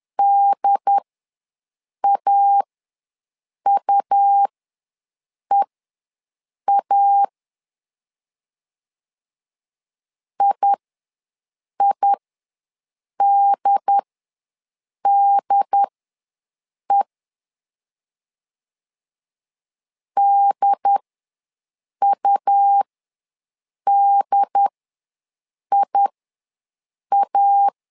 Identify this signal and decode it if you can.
Morse code, heard as a keyed tone, about 10 wpm, spaced as for 6 wpm, 800 Hz. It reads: DAUEA IIDDE DUDIA
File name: LEZIONI / 2 UHDN / 2 UHD ws 10wpm 2058